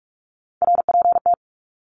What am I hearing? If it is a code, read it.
Morse code, heard as a keyed tone, about 44 wpm, 700 Hz. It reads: RPT
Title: Abbreviations 45wpm